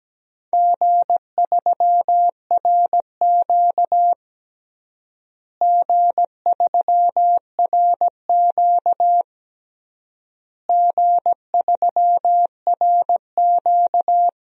Morse code, heard as a keyed tone, about 17 wpm, 700 Hz. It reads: G3RQ G3RQ G3RQ
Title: DXCC-100 Call Signs - 4 Characters - Repeated 3x - 17wpm